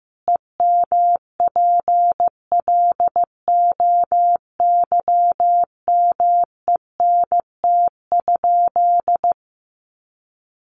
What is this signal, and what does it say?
Morse code, heard as a keyed tone, about 15 wpm, 700 Hz. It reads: EMPLOYMENT?
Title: Sentences from Top 1500 Words 15wpm